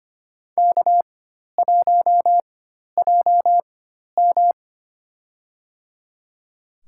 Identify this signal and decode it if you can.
Morse code, heard as a keyed tone, about 25 wpm, spaced as for 12 wpm, 700 Hz. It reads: K1JM
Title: USA Call Signs - 4 Characters - 4x Character Spacing - 25wpm